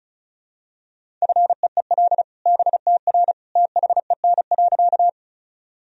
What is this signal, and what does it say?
Morse code, heard as a keyed tone, about 36 wpm, 700 Hz. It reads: FEEL BTR THEN.
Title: Sentences from Top 400 Words and Abbreviations 35wpm